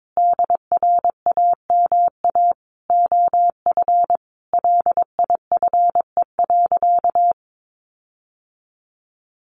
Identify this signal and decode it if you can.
Morse code, heard as a keyed tone, about 22 wpm, 700 Hz. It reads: DRAMA OF LIFE.
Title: Sentences from Top 1500 Words 22wpm